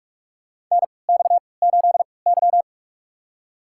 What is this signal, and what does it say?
Morse code, heard as a keyed tone, about 48 wpm, spaced as for 26 wpm, 700 Hz. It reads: NX8Y
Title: USA Call Signs - 4 Characters - 3x Character Spacing - 45wpm